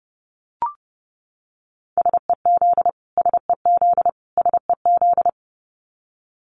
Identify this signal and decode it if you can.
Morse code, heard as a keyed tone, about 30 wpm, 700 Hz. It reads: SEZ SEZ SEZ
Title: Abbreviations Repeated 3x - 30wpm